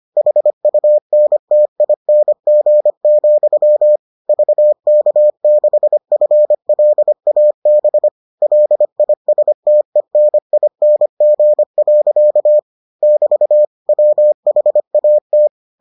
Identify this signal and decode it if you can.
Morse code, heard as a keyed tone, about 25 wpm, 600 Hz. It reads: HUNTING, VK6FLAB LISTENING. <BT> WHAT